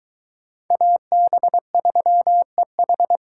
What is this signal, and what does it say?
Morse code, heard as a keyed tone, about 22 wpm, 700 Hz. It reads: AB3EH